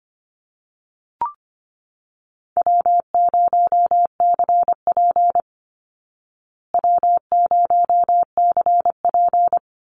Morse code, heard as a keyed tone, about 25 wpm, 700 Hz. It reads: W0CP W0CP
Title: Easy Call Signs Repeated 3x - 25wpm